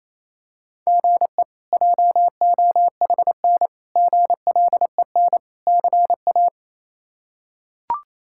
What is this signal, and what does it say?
Morse code, heard as a keyed tone, about 28 wpm, 700 Hz. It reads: GE JOHN GLEN CA
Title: K1USN SST QSO Practice - 28wpm